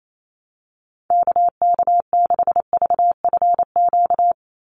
Morse code, heard as a keyed tone, about 28 wpm, 700 Hz. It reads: KK6VFQ
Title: USA Call Signs 28wpm